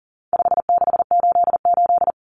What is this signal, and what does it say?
Morse code, heard as a keyed tone, about 40 wpm, 700 Hz. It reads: 5688
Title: Common Cut Numbers - 4-Digit 40wpm